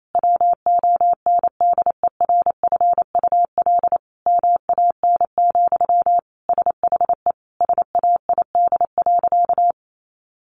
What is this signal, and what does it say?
Morse code, heard as a keyed tone, about 28 wpm, 700 Hz. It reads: WONDERFUL MAN, SHE SAID.
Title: Sentences from Top 1100 Words 28wpm